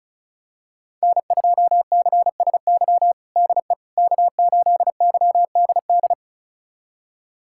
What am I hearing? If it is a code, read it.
Morse code, heard as a keyed tone, about 36 wpm, 700 Hz. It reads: N2CSY DE K8YDD
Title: USA Callsign DE Callsign - 5 Characters - 35wpm